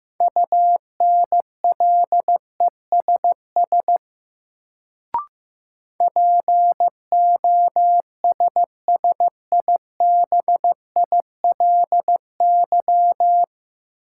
Morse code, heard as a keyed tone, about 15 wpm, 700 Hz. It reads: UNLESS POSSIBILY
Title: Sentences from Top 1300 Words - New Word Drill Alt - 15wpm